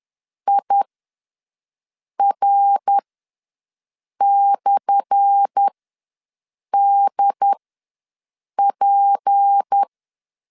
Morse code, heard as a keyed tone, about 10 wpm, spaced as for 5 wpm, 800 Hz. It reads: IR/DP
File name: LEZIONI / 10 SIGNS / 10C DOT ws 10wpm 4213